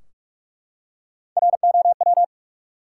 Morse code, heard as a keyed tone, about 44 wpm, 700 Hz. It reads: ROW
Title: Three Letter Words - 45wpm